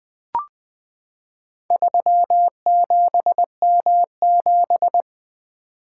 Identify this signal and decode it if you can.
Morse code, heard as a keyed tone, about 20 wpm, 700 Hz. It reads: 37M7
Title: Four Letter-Number Combinations - 20wpm